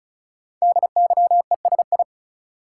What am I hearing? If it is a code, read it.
Morse code, heard as a keyed tone, about 36 wpm, 690 Hz. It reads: DYESI
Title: Code Groups - 5 Letters - 35wpm